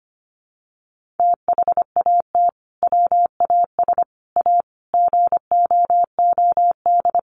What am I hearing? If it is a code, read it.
Morse code, heard as a keyed tone, about 25 wpm, 700 Hz. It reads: THAT WAS A GOOD
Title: Sentences from Top 200 Words 25wpm